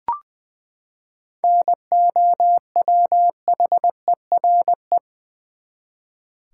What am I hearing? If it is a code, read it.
Morse code, heard as a keyed tone, about 20 wpm, 700 Hz. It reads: NOWHERE
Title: Seven Letter Words - 20wpm